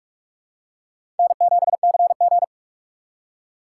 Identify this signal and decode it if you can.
Morse code, heard as a keyed tone, about 44 wpm, 700 Hz. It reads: N7CG
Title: USA Call Signs - 4 Characters - 2x Character Spacing - 45wpm